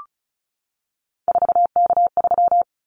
Morse code, heard as a keyed tone, about 36 wpm, 700 Hz. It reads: 4K3
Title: Callsign Prefixes for Top 100 DXCC - 35wpm